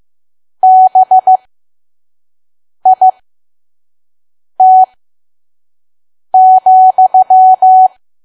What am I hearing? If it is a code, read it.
Morse code, heard as a keyed tone, about 15 wpm, spaced as for 5 wpm, 750 Hz. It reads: BIT,
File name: Archive / 5 WPM files / PM